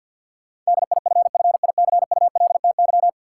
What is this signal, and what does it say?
Morse code, heard as a keyed tone, about 52 wpm, 700 Hz. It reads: DIFFICULTY
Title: Sentences from Top 1500 Words - New Word Drill Alt - 50wpm